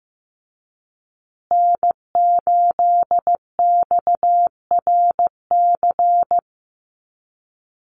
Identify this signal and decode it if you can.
Morse code, heard as a keyed tone, about 15 wpm, 700 Hz. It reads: N8XRC
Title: USA Call Signs Repeated 3x 15wpm